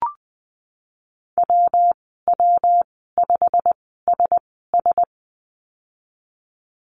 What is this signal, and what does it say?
Morse code, heard as a keyed tone, about 20 wpm, spaced as for 15 wpm, 700 Hz. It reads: WW5SS
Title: 238 Lesson 38g - Letter q - Call Signs 20wpm